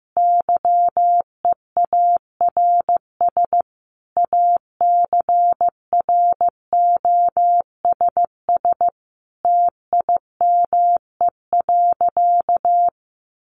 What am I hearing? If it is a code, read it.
Morse code, heard as a keyed tone, about 15 wpm, 700 Hz. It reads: YEARS ACROSS TIME.
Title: A Sound Of Thunder 15wpm